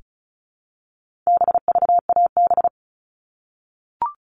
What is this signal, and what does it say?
Morse code, heard as a keyed tone, about 34 wpm, 700 Hz. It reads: BVAB
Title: All Cut Numbers - 4-Digit 35wpm